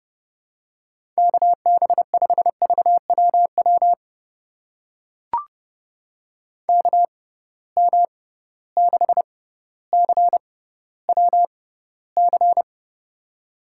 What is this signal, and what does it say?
Morse code, heard as a keyed tone, about 30 wpm, spaced as for 15 wpm, 700 Hz. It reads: KB5VWW KM6CWC
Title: USA Call Signs - 6 Characters - 6x Character Spacing - 30wpm